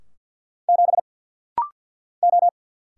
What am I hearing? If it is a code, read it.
Morse code, heard as a keyed tone, about 50 wpm, 700 Hz. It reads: 7 O
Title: Single Letter-Number - Rapid-Fire 50wpm